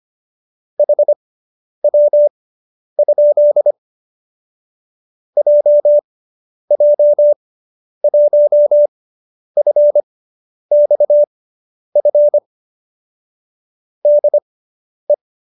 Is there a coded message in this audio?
Yes: Morse code from 0.8 to 15.1 s, heard as a keyed tone, about 25 wpm, spaced as for 10 wpm, 590 Hz. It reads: HW? JJ1FXF DE